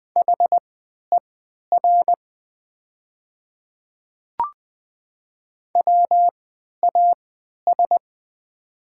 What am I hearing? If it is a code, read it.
Morse code, heard as a keyed tone, about 20 wpm, spaced as for 11 wpm, 700 Hz. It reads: HER WAS
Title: 3x Character Spacing - Top 100 Words - 20wpm